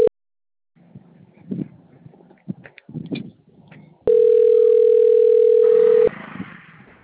Ringback tone (North America), 440 + 480 Hz.
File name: rg-1600-unknown- - - .61064